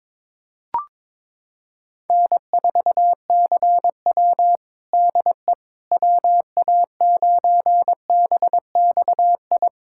Morse code, heard as a keyed tone, about 22 wpm, 700 Hz. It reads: N4CW DE WA9BXI
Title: Practice QSO - Exchange 4 - Ending - 22wpm